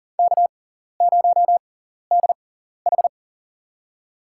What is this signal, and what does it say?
Morse code, heard as a keyed tone, about 40 wpm, spaced as for 14 wpm, 700 Hz. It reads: K0DH
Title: USA Call Signs - 6x Character Spacing - 40wpm